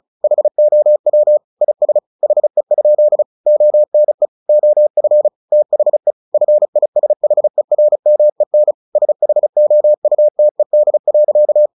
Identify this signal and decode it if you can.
Morse code, heard as a keyed tone, about 34 wpm, 600 Hz. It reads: HOW IS HE? ONE OF THE FISHERMEN SHOUTED.
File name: the old man and the sea 2 35wpm